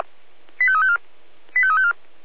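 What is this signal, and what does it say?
Five-tone selective call (ZVEI), heard as quick runs of five tones, tones of about 70 ms. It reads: 85354 85354
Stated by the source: ZVEI five-tone call 85243 received on an SDR.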